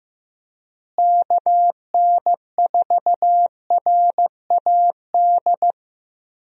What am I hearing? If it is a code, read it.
Morse code, heard as a keyed tone, about 15 wpm, 700 Hz. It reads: KN4RAD
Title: USA Call Signs - 6 Characters - Repeated 2x - 15wpm